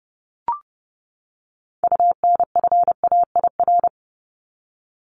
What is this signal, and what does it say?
Morse code, heard as a keyed tone, about 30 wpm, 700 Hz. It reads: UNFAIR
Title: Top 1751-2000 Words 30wpm